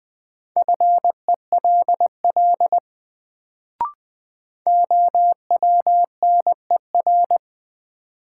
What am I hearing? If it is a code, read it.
Morse code, heard as a keyed tone, about 21 wpm, 700 Hz. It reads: FELL OWNER
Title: Sentences from Top 1600 Words - New Word Drill Alt - 20wpm